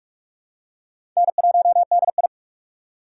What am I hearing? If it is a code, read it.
Morse code, heard as a keyed tone, about 45 wpm, 700 Hz. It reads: N1DI
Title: USA Call Signs - 4 Characters - 3x Character Spacing - 45wpm